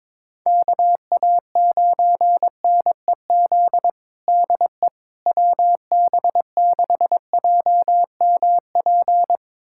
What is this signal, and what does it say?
Morse code, heard as a keyed tone, about 22 wpm, 700 Hz. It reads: KA9NEZ DE WB6JMP
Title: USA Callsign DE Callsign 22wpm